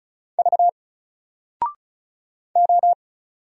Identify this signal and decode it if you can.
Morse code, heard as a keyed tone, about 36 wpm, 700 Hz. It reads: V O
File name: Single-Letter-Number-35wpm